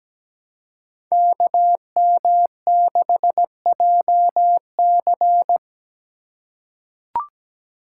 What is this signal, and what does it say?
Morse code, heard as a keyed tone, about 17 wpm, 700 Hz. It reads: KM6JC
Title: Code Groups - 5 Letter:Numbers - 17wpm